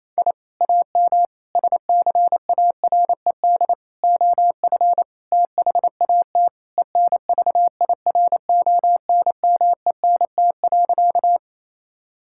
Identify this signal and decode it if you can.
Morse code, heard as a keyed tone, about 27 wpm, 700 Hz. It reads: I AM SCARED OF THAT ENVIRONMENT.
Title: Sentences from Top 1000 Words - Review Part 5 - 28wpm